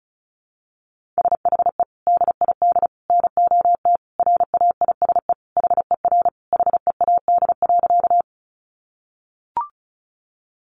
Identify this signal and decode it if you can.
Morse code, heard as a keyed tone, about 34 wpm, 700 Hz. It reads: SHE DID NOT RAISE HER HEAD.
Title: Sentences from Top 1400 Words 35wpm